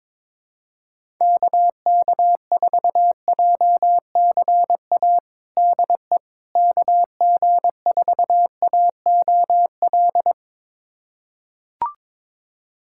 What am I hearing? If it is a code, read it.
Morse code, heard as a keyed tone, about 22 wpm, 700 Hz. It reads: KK4JCA DE KG4AOL
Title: USA Callsign DE Callsign - 6 Characters - 22wpm